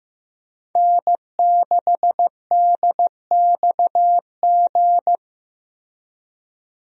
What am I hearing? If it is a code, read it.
Morse code, heard as a keyed tone, about 15 wpm, 700 Hz. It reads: N6DXG
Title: USA Call Signs - 5 Characters - Repeated 3x - 15wpm